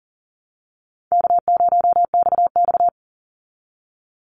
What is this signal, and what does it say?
Morse code, heard as a keyed tone, about 40 wpm, 700 Hz. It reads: K0XX
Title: Easy Call Signs Repeated 2x - 40wpm